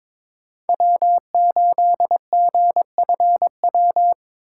Morse code, heard as a keyed tone, about 22 wpm, 700 Hz. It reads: W8GFW
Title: USA Call Signs - 5 Characters - 4x Character Spacing - 22wpm